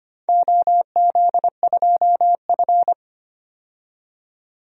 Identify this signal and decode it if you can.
Morse code, heard as a keyed tone, about 25 wpm, 700 Hz. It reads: OZ2F
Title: DXCC-100 Call Signs - 4 Characters - 25wpm